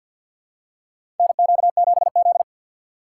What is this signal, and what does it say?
Morse code, heard as a keyed tone, about 53 wpm, 700 Hz. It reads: NQ7Z
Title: USA Call Signs - 4 Characters - 2x Character Spacing - 50wpm